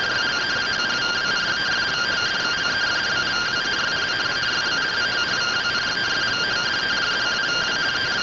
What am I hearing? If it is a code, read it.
Two-tone teleprinter signal, about 170 Hz shift.